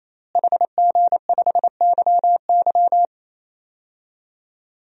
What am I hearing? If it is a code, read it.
Morse code, heard as a keyed tone, about 28 wpm, 700 Hz. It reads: HG5YY
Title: DXCC-100 Call Signs - 5 Characters - Repeated 3x - 28wpm